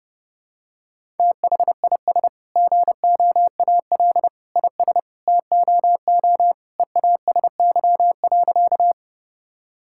Morse code, heard as a keyed tone, about 30 wpm, 700 Hz. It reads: THIS GOAL IS TOO EASY.
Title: Sentences from Top 1000 Words - Review Part 5 - 30wpm